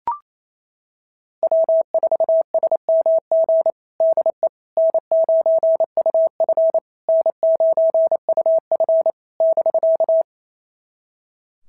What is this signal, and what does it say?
Morse code, heard as a keyed tone, about 25 wpm, 650 Hz. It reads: W4SMG DE N9UF N9UF <BK>
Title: SOTA QSO Practice 28wpm